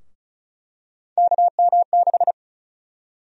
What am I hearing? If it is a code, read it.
Morse code, heard as a keyed tone, about 36 wpm, 700 Hz. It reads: KM6
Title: USA Call Sign Prefixes - Repeated 2x - 35wpm